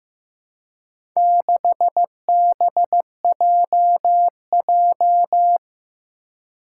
Morse code, heard as a keyed tone, about 15 wpm, 700 Hz. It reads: 6BJJ
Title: Four Challenging Letter-Number Combinations 15wpm